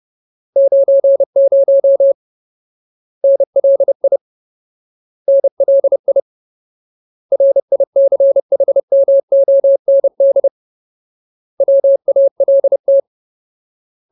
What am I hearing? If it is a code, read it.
Morse code, heard as a keyed tone, about 30 wpm, 550 Hz. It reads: 90 NLI NLI RICHMOND WALT
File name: Ultimate Word List / 30WPM Ultimate